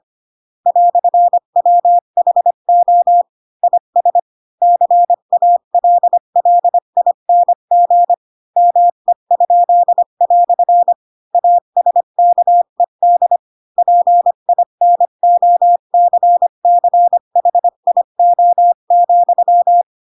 Morse code, heard as a keyed tone, about 25 wpm, 700 Hz. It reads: "WHO IS CALLING ME?" ASKED PINOCCHIO,
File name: Pinocchio / Pinocchio ch4 20 25 ik1hge